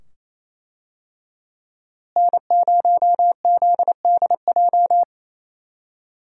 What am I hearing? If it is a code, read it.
Morse code, heard as a keyed tone, about 28 wpm, 700 Hz. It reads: N0ZDJ
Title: USA Call Signs - 5 Characters - Repeated 2x - 28wpm